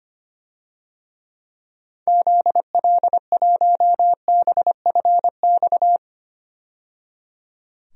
Morse code, heard as a keyed tone, about 25 wpm, 700 Hz. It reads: ZL1BFX
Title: DXCC-100 Call Signs - 6 Characters - 25wpm